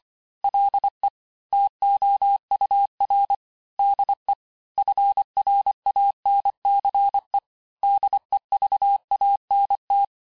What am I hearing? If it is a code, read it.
Morse code, heard as a keyed tone, about 25 wpm, 800 Hz. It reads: LE TOUR DE FRANCE DEVANT
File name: cours-du-lundi files / cours-1589-texte-clair-vitesse-18mpm-puis-23mpm